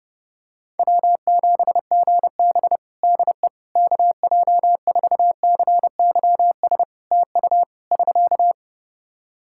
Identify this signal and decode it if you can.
Morse code, heard as a keyed tone, about 32 wpm, 700 Hz. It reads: W7GB DE KJ4CYS TU <SK>
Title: Practice QSO - Exchange 4 - Ending - 30wpm